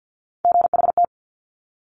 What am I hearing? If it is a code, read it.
Morse code, heard as a keyed tone, about 50 wpm, 700 Hz. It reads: GHT
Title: Top Three Letter Combinations in Words 50wpm